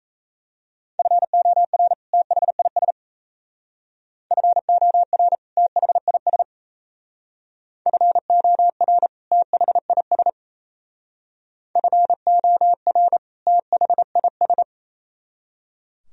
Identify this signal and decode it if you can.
Morse code, heard as a keyed tone, about 33 wpm, 700 Hz. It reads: FOR THIS FOR THIS FOR THIS FOR THIS
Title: Speed-Racing - Sets of 2 Words 28wpm